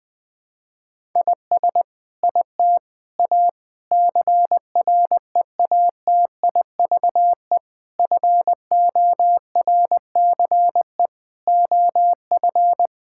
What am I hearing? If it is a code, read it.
Morse code, heard as a keyed tone, about 20 wpm, 700 Hz. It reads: IS IT A CREATIVE FORCE OF
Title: Sentences from Top 1000 Words - Review Part 5 - 20wpm